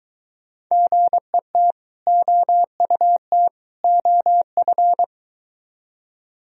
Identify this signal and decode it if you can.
Morse code, heard as a keyed tone, about 23 wpm, 700 Hz. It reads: GET OUT OF